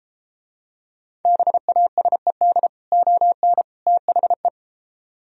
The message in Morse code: BASED ON THE

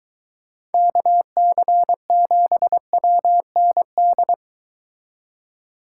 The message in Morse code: KC7WND

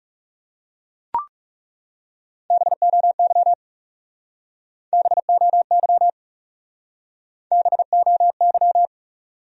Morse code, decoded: BOY BOY BOY

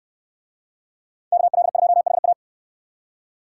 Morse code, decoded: KL2HA

